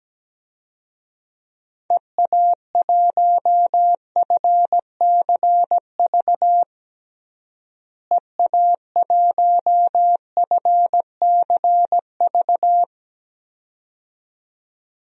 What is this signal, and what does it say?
Morse code, heard as a keyed tone, about 17 wpm, 700 Hz. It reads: EA1FCV EA1FCV
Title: DXCC-100 Call Signs - 6 Characters - Repeated 2x - 17wpm